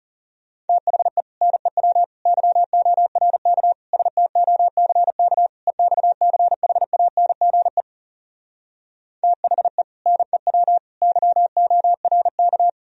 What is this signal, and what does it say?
Morse code, heard as a keyed tone, about 34 wpm, 700 Hz. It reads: THE NEW YORK STOCK EXCHANGE THE NEW YORK
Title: Speed-Racing - Sets of 5 Words 30wpm